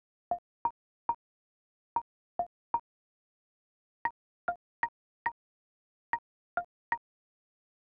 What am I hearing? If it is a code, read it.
There is no code.